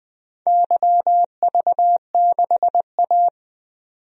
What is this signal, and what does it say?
Morse code, heard as a keyed tone, about 20 wpm, 700 Hz. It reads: YV6A
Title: DXCC-100 Call Signs - 4 Characters - 20wpm